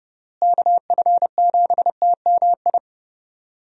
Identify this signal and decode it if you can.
Morse code, heard as a keyed tone, about 32 wpm, 700 Hz. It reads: KF7TMI
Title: USA Call Signs - 6 Characters - Repeated 3x - 30wpm